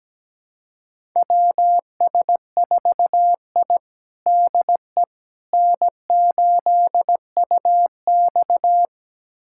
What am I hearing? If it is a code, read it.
Morse code, heard as a keyed tone, about 17 wpm, 700 Hz. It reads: WS4I DE N8UX
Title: USA Callsign DE Callsign - 4 Characters - 17wpm